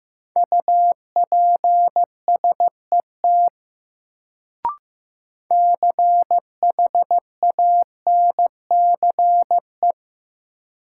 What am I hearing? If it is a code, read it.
Morse code, heard as a keyed tone, about 15 wpm, 700 Hz. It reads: UPSET CHANCE
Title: Sentences from Top 800 Words - New Word Drill - 15wpm